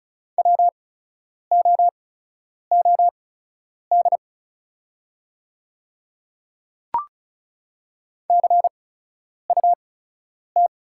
Morse code, heard as a keyed tone, about 36 wpm, spaced as for 10 wpm, 700 Hz. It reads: WOOD CUT
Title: 8x Character Spacing - Top 300 Words - 35wpm